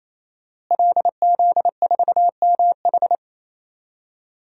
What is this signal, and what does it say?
Morse code, heard as a keyed tone, about 27 wpm, 700 Hz. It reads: LZ4MH
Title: DXCC-100 Call Signs - 5 Characters - Repeated 3x - 28wpm